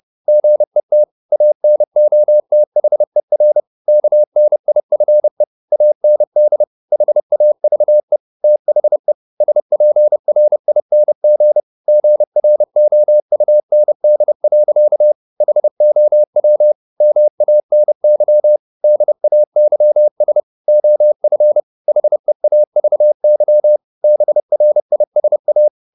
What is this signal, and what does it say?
Morse code, heard as a keyed tone, about 30 wpm, 600 Hz. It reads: GET ANOTHER KNIFE AND HAVE THE SPRING GROUND. HOW MANY DAYS OF HEAVY BRISA